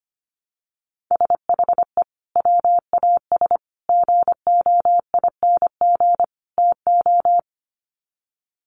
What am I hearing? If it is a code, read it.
Morse code, heard as a keyed tone, about 25 wpm, 700 Hz. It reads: SHE WAS GOING TO
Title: Sets of 4 Words 25wpm